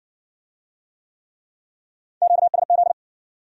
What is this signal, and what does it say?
Morse code, heard as a keyed tone, about 56 wpm, 700 Hz. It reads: QSZ